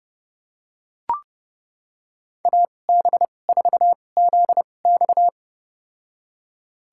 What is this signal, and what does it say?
Morse code, heard as a keyed tone, about 32 wpm, spaced as for 22 wpm, 700 Hz. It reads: AB4ZX
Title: USA Call Signs - 5 Characters - 2x Character Spacing - 30wpm